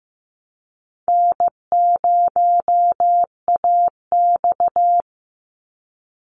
Morse code, heard as a keyed tone, about 15 wpm, 700 Hz. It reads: N0AX
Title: Call Signs Repeated 2x - 15wpm